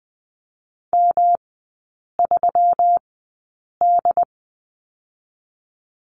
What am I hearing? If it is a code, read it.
Morse code, heard as a keyed tone, about 20 wpm, spaced as for 8 wpm, 700 Hz. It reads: M3D
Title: 121 Lesson 20e - Number 3 - Three Characters 20wpm